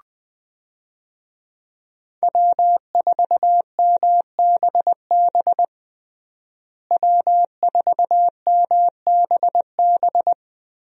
Morse code, heard as a keyed tone, about 20 wpm, 700 Hz. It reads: W4MBB W4MBB
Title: USA Call Signs - 5 Characters - Repeated 2x - 20wpm